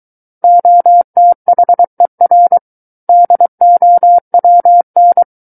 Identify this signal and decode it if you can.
Morse code, heard as a keyed tone, about 23 wpm, 700 Hz. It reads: OTHER DOWN
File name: Book One / 14-Chapter 14 - 23 WPM